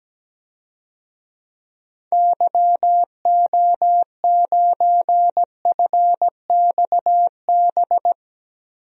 Morse code, heard as a keyed tone, about 17 wpm, 700 Hz. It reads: YO9FXB
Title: DXCC-100 Call Signs - 6 Characters - Repeated 3x - 17wpm